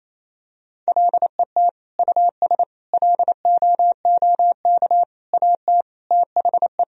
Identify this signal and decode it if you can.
Morse code, heard as a keyed tone, about 28 wpm, 700 Hz. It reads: LET US LOOK AT THE
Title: Sentences from Top 1000 Words - Review Part 2 - 28wpm